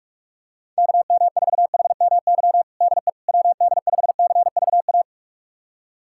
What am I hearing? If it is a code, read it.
Morse code, heard as a keyed tone, about 41 wpm, 700 Hz. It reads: KM4HMY DE WD5CVA
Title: USA Callsign DE Callsign - 6 Characters - 45wpm